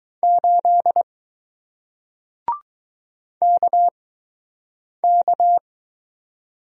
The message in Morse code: 8 KK

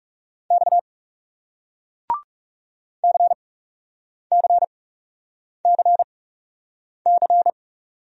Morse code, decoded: X C C C C

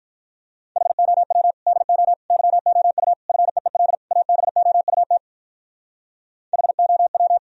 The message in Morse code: HOW DO YOU FEEL ABOUT HOW

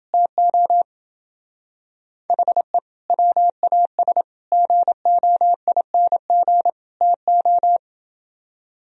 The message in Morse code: TO HE WAS GOING TO

